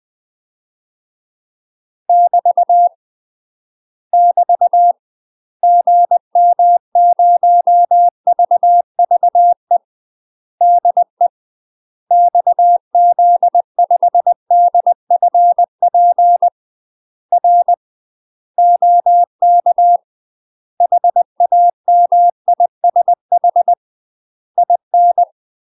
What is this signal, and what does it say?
Morse code, heard as a keyed tone, about 20 wpm, 700 Hz. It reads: <BT> <BT> GM0VVE DE XZ5DFP R OK HAMISH IN
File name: UK MASTER QSOs TRAINING / 20WPM / UK MASTER QSOs 0025